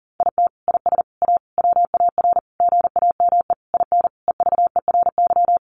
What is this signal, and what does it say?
Morse code, heard as a keyed tone, about 40 wpm, 700 Hz. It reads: IT IS A WAR GAME IN EVERY
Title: Sentences from Top 1500 Words 40wpm